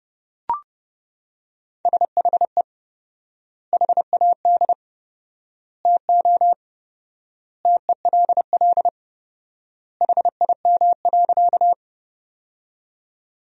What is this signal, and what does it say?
Morse code, heard as a keyed tone, about 32 wpm, 700 Hz. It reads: SHE HAD TO TELL HIM.
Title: 4x Word-Spacing - Sentences from Top 200 Words - 30wpm